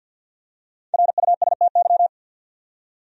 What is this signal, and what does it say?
Morse code, heard as a keyed tone, about 53 wpm, 700 Hz. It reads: RUSTY
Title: USA Common Names - Max 5 Letters - 50wpm